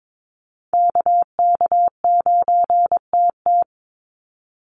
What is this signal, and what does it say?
Morse code, heard as a keyed tone, about 22 wpm, 700 Hz. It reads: KK9TT
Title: USA Call Signs 22wpm